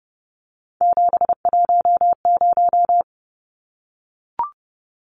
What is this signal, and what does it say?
Morse code, heard as a keyed tone, about 30 wpm, 700 Hz. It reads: 710
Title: Numbers - 3-Digit 30wpm